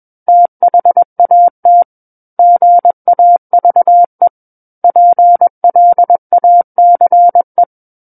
Morse code, heard as a keyed tone, about 21 wpm, 700 Hz. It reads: THAT GAVE PLACE
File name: Book One / 12-Chapter 12 - 21 WPM